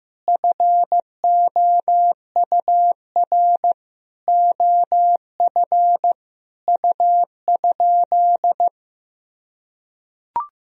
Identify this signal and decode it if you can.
Morse code, heard as a keyed tone, about 15 wpm, 700 Hz. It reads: FOUR OF U?
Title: Sentences from Top 300 Words and Abbreviations 15wpm